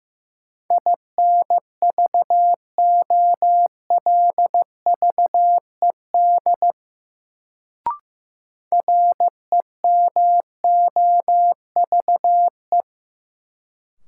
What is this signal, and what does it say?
Morse code, heard as a keyed tone, about 16 wpm, 700 Hz. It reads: INVOLVED REMOVE
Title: Sentences from Top 800 Words - New Word Drill - 15wpm